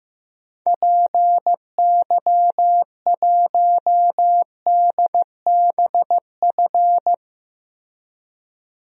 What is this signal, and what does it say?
Morse code, heard as a keyed tone, about 15 wpm, 700 Hz. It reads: PY1DBF
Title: DXCC-100 Call Signs - 6 Characters - Repeated 2x - 15wpm